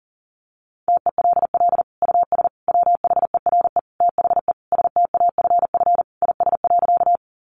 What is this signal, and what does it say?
Morse code, heard as a keyed tone, about 40 wpm, 700 Hz. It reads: TELL US WHERE THE STAFF IS.
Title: Sentences from Top 1200 Words 40wpm